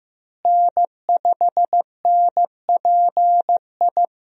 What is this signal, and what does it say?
Morse code, heard as a keyed tone, about 15 wpm, 700 Hz. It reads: N5NPI
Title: USA Call Signs - 5 Characters - Repeated 2x - 15wpm